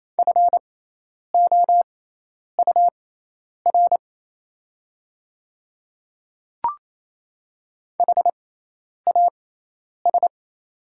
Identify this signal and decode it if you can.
Morse code, heard as a keyed tone, about 28 wpm, spaced as for 10 wpm, 700 Hz. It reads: FOUR HAS
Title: 6x Character Spacing - Top 300 Words - 28wpm